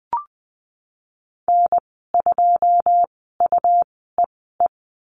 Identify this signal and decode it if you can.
Morse code, heard as a keyed tone, about 20 wpm, spaced as for 15 wpm, 700 Hz. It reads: N2UEE
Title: 128 Lesson 21g - Number 6 - Call Signs 20wpm